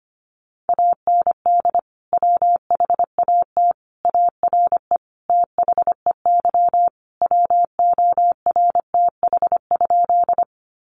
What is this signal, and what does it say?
Morse code, heard as a keyed tone, about 25 wpm, 700 Hz. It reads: AND WHAT ARE THEY WORTH?
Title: Sentences from Top 800 Words 25wpm